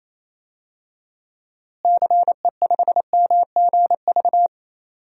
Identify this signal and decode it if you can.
Morse code, heard as a keyed tone, about 28 wpm, 700 Hz. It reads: CE5MGV
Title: DXCC-100 Call Signs - 6 Characters - Repeated 3x - 28wpm